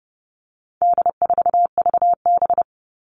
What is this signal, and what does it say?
Morse code, heard as a keyed tone, about 30 wpm, 700 Hz. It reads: D4VB
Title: All Cut Numbers - 4-Digit 30wpm